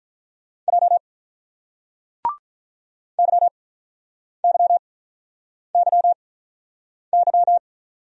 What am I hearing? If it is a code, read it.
Morse code, heard as a keyed tone, about 46 wpm, spaced as for 9 wpm, 700 Hz. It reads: J YYYY